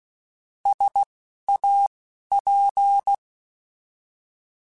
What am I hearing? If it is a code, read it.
Morse code, heard as a keyed tone, about 16 wpm, spaced as for 11 wpm, 800 Hz. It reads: SAP
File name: Alphabetical / Alphabetical mp3s / Ss @10